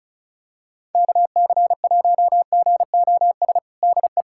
Morse code, heard as a keyed tone, about 36 wpm, 700 Hz. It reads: KC1GOS DE
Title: USA Callsign DE Callsign - 6 Characters - 35wpm